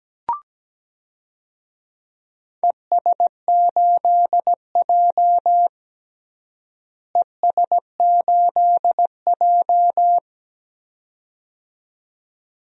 Morse code, heard as a keyed tone, about 17 wpm, 700 Hz. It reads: ES8J ES8J
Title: DXCC-100 Call Signs - 4 Characters - Repeated 2x - 17wpm